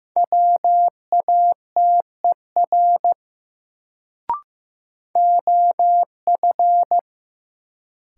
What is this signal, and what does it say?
Morse code, heard as a keyed tone, about 15 wpm, 700 Hz. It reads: WATER OF
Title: Sentences from Top 100 Words - New Word Drill - 15wpm